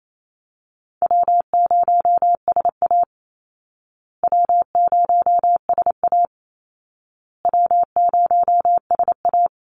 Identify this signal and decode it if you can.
Morse code, heard as a keyed tone, about 28 wpm, 700 Hz. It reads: W0SA W0SA W0SA
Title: Easy Call Signs Repeated 3x - 28wpm